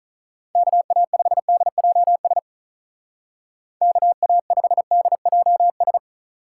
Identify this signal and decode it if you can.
Morse code, heard as a keyed tone, about 44 wpm, 700 Hz. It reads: KA5DJS KA5DJS